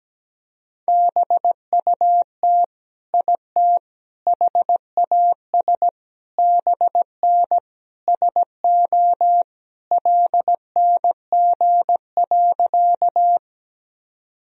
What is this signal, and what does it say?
Morse code, heard as a keyed tone, about 17 wpm, 700 Hz. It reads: BUT IT HAS BN SO LNG.
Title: Sentences from Top 100 Words and Abbreviations 17wpm